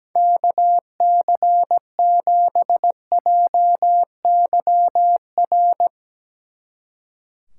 Morse code, heard as a keyed tone, about 18 wpm, 700 Hz. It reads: KC7JYR
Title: Speed Racing - USA 2x3 Call Signs 17wpm